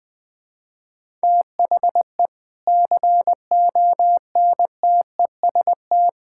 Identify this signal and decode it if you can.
Morse code, heard as a keyed tone, about 21 wpm, 700 Hz. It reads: THE CONTEST
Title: Sentences from Top 1600 Words 20wpm